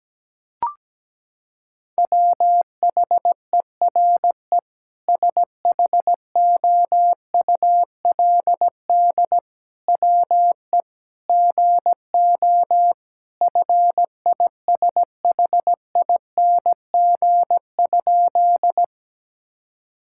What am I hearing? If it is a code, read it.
Morse code, heard as a keyed tone, about 17 wpm, 700 Hz. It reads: WHERE SHOULD WE GO FISHING?
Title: Sentences from Top 1000 Words - Review Part 5 - 17wpm